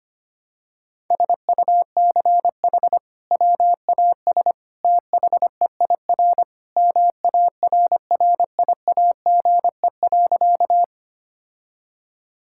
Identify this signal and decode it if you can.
Morse code, heard as a keyed tone, about 25 wpm, 700 Hz. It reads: SUCH WAS THEIR MARRIAGE.
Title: Sentences from Top 1000 Words - Review Part 5 - 25wpm